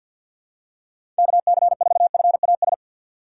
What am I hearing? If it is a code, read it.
Morse code, heard as a keyed tone, about 47 wpm, 700 Hz. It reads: KC4FAS
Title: USA Call Signs - 6 Characters - 4x Character Spacing - 50wpm